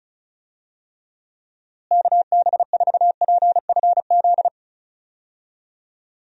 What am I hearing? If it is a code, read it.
Morse code, heard as a keyed tone, about 36 wpm, 700 Hz. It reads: KB4PFZ
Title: USA Call Signs - 6 Characters - Repeated 2x - 35wpm